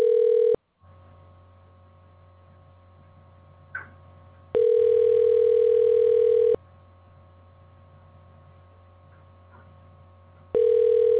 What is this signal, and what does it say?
Ringback tone (North America), 440 + 480 Hz.